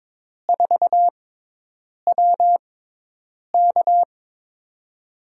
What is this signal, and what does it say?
Morse code, heard as a keyed tone, about 22 wpm, spaced as for 8 wpm, 700 Hz. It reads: 4WK
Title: USA Call Signs - 6x Character Spacing - 22wpm